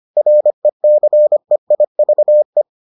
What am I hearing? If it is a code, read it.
Morse code, heard as a keyed tone, about 25 wpm, 600 Hz. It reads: RECEIVE